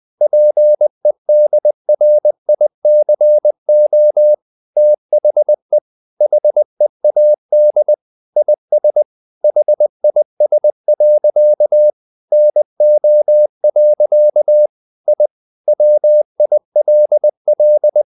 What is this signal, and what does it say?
Morse code, heard as a keyed tone, about 20 wpm, 600 Hz. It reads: PEDRICO THE HEAD IS HIS. NO. I WILL
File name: the old man and the sea part 2 20wpm